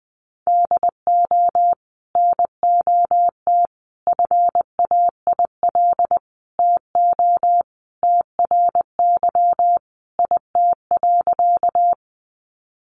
Morse code, heard as a keyed tone, about 20 wpm, 700 Hz. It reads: DO NOT FAIL TO TRY IT.
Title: Sentences from Top 1100 Words 20wpm